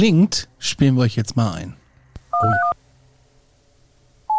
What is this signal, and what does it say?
DTMF, heard as telephone touch tones, tones of about 110 ms, with speech. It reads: 1231